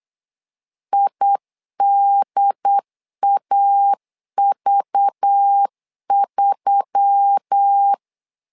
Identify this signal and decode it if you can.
Morse code, heard as a keyed tone, about 8 wpm, 800 Hz. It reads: IDAV3
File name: LEZIONI / 5 LBVC / 5 LBV 8wpm 2837